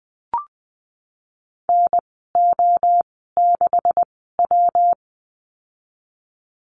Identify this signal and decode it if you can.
Morse code, heard as a keyed tone, about 20 wpm, spaced as for 15 wpm, 700 Hz. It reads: NO6W
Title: 173 Lesson 28g - Number 9 - Call Signs 20wpm